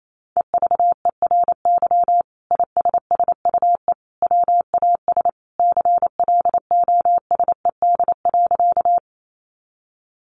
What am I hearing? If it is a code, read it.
Morse code, heard as a keyed tone, about 28 wpm, 700 Hz. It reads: EVERY ISSUE WAS CLOSED.
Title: Sentences from Top 1400 Words 28wpm